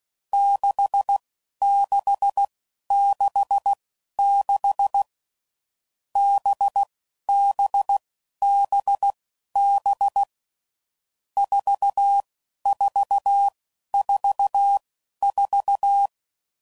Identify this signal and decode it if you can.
Morse code, heard as a keyed tone, about 16 wpm, spaced as for 11 wpm, 800 Hz. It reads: 6666 BBBB 4444
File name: Groups / Basic Letter Groups / VB46